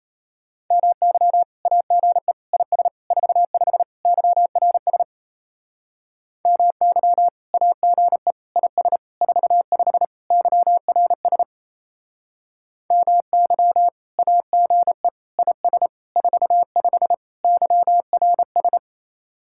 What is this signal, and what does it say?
Morse code, heard as a keyed tone, about 34 wpm, 700 Hz. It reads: MY AGE IS 45 YRS MY AGE IS 45 YRS MY AGE IS 45 YRS
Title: Speed-Racing Common QSO Phrases - 28wpm